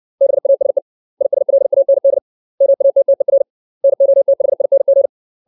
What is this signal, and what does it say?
Morse code, heard as a keyed tone, about 60 wpm, 550 Hz. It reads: BASE ISLAND MATTER NOTHING
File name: 632 words / 632 Words 60WPM